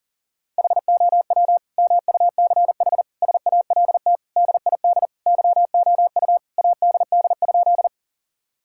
Morse code, heard as a keyed tone, about 37 wpm, 700 Hz. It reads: HOW MUCH SALT DID YOU ADD?
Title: Sentences from Top 1000 Words - Review Part 5 - 40wpm